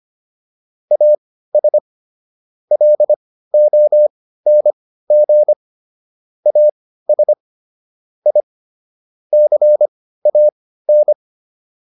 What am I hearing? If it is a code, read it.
Morse code, heard as a keyed tone, about 25 wpm, spaced as for 15 wpm, 600 Hz. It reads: AS LONG AS I CAN
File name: WD404 15